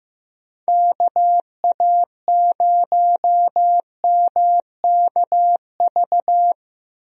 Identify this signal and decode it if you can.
Morse code, heard as a keyed tone, about 15 wpm, 700 Hz. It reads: KA0MKV
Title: USA Call Signs - 6 Characters - 15wpm